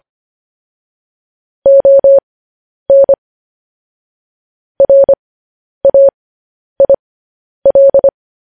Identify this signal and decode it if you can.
Morse code, heard as a keyed tone, about 25 wpm, spaced as for 10 wpm, 550 Hz. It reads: ON RAIL
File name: podcast / News headlines-1x-25x10